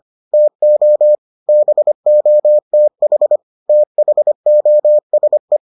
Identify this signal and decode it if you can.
Morse code, heard as a keyed tone, about 25 wpm, 600 Hz. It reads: TO BOTH THOSE